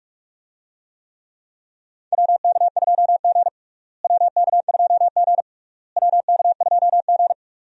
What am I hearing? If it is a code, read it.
Morse code, heard as a keyed tone, about 41 wpm, 700 Hz. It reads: WK2G WK2G WK2G
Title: USA Call Signs - 4 Characters - Repeated 3x - 45wpm